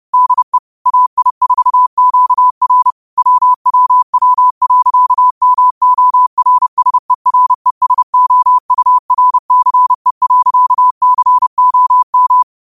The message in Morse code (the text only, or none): DE AI4QR WWW.MORSERESOURCE.COM